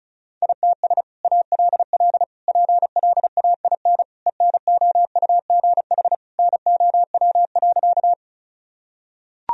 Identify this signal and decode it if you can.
Morse code, heard as a keyed tone, about 36 wpm, 700 Hz. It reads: ITS ALL PLAIN ENOUGH NOW.
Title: Sentences from Top 1000 Words - Review Part 2 - 35wpm